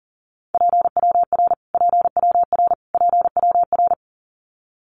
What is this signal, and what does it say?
Morse code, heard as a keyed tone, about 40 wpm, 700 Hz. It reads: PWR PWR PWR
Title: Abbreviations Repeated 3x - 40wpm